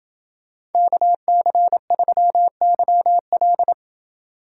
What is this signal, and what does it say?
Morse code, heard as a keyed tone, about 28 wpm, 700 Hz. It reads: KC3YL